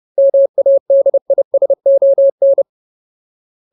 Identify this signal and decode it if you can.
Morse code, heard as a keyed tone, about 30 wpm, 550 Hz. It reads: MADISON
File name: Ultimate Word List / 30WPM Ultimate